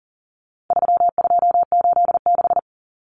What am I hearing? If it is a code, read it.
Morse code, heard as a keyed tone, about 40 wpm, 700 Hz. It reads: 3286
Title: Common Cut Numbers - 4-Digit 40wpm